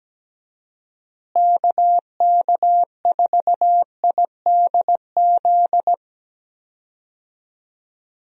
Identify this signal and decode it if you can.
Morse code, heard as a keyed tone, about 17 wpm, 700 Hz. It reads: KK4IDZ
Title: USA Call Signs - 6 Characters - Repeated 2x - 17wpm